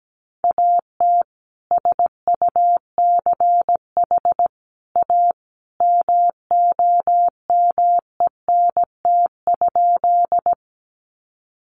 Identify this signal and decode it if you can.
Morse code, heard as a keyed tone, about 17 wpm, 700 Hz. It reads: AT SUCH A MOMENT?
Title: Sentences from Top 1000 Words 17wpm